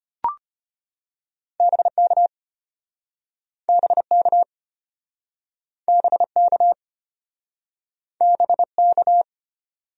Morse code, heard as a keyed tone, about 33 wpm, 700 Hz. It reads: BK BK BK BK